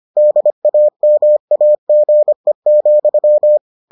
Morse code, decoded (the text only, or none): DAMAGE,